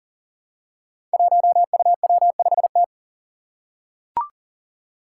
Morse code, heard as a keyed tone, about 38 wpm, 700 Hz. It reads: 1UW5T
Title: Five Challenging Letter-Number Combinations 40wpm